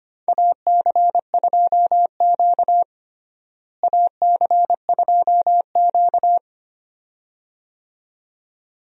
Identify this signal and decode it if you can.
Morse code, heard as a keyed tone, about 25 wpm, 700 Hz. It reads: AC2Q AC2Q
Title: USA Call Signs - 4 Characters - Repeated 2x - 25wpm